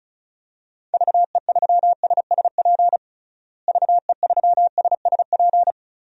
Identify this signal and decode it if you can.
Morse code, heard as a keyed tone, about 34 wpm, 700 Hz. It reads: VE3SSP VE3SSP
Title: DXCC-100 Call Signs - 6 Characters - Repeated 3x - 35wpm